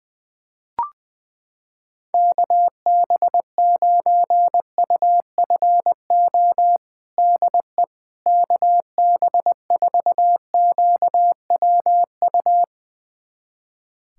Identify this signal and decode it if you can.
Morse code, heard as a keyed tone, about 20 wpm, 700 Hz. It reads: KB9UFO DE KB4QWU
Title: USA Callsign DE Callsign - 6 Characters - 20wpm